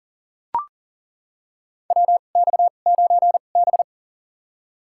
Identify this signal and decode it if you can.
Morse code, heard as a keyed tone, about 40 wpm, spaced as for 29 wpm, 700 Hz. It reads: WX9B
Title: USA Call Signs - 2x Character Spacing - 40wpm